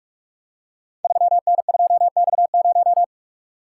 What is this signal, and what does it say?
Morse code, heard as a keyed tone, about 44 wpm, 700 Hz. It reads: 3N2X0
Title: Code Groups - 5 Letter:Numbers - 45wpm